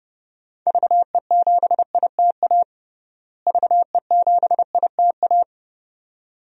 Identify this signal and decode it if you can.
Morse code, heard as a keyed tone, about 32 wpm, 700 Hz. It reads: VE7ITA VE7ITA
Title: DXCC-100 Call Signs - 6 Characters - Repeated 2x - 30wpm